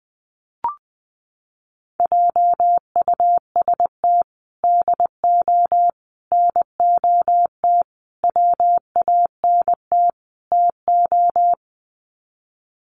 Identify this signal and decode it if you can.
Morse code, heard as a keyed tone, about 20 wpm, 700 Hz. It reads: JUST DO NOT WANT TO
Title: Sets of 5 Words Encore 20wpm